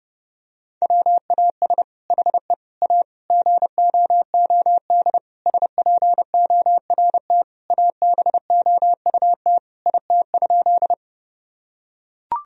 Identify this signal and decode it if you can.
Morse code, heard as a keyed tone, about 32 wpm, 700 Hz. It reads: WAS HE A GOOD SPORT ABOUT IT?
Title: Sentences from Top 1000 Words - Review Part 4 - 30wpm